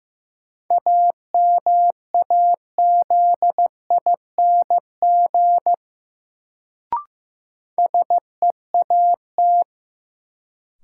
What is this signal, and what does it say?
Morse code, heard as a keyed tone, about 15 wpm, 700 Hz. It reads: AMAZING SEAT
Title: Sentences from Top 1500 Words - New Word Drill - 15wpm